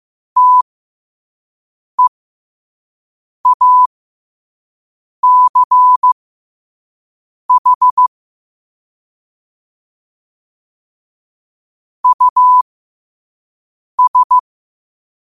Morse code, heard as a keyed tone, about 15 wpm, spaced as for 5 wpm, 1000 Hz. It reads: TEACH US
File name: qotdfiles / 5wpm